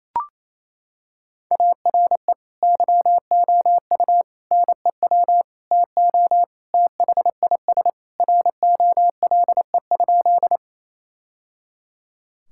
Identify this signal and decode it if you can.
Morse code, heard as a keyed tone, about 28 wpm, 700 Hz. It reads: ARE YOU NEW TO THIS ROLE?
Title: Sentences from Top 1000 Words - Review Part 6 - 28wpm